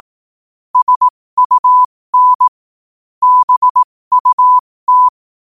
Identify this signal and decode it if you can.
Morse code, heard as a keyed tone, about 18 wpm, spaced as for 15 wpm, 1000 Hz. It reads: SUN BUT
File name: qotdfiles / 15wpm